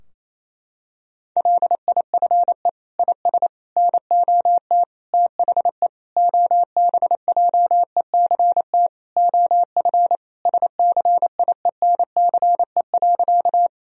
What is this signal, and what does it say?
Morse code, heard as a keyed tone, about 27 wpm, 700 Hz. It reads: LIFE IS NOT THE OBJECT OF SCIENCE.
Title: Sentences from Top 1000 Words - Review Part 3 - 28wpm